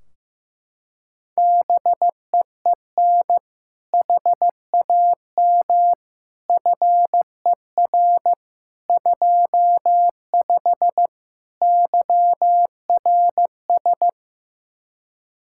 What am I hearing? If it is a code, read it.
Morse code, heard as a keyed tone, about 15 wpm, 700 Hz. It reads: BEEN HAM FER 25 YRS
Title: Practice QSO - Exchange 3 - More Info - 15wpm